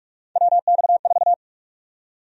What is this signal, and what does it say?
Morse code, heard as a keyed tone, about 41 wpm, 700 Hz. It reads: WX4
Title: USA Call Sign Prefixes - Repeated 2x - 45wpm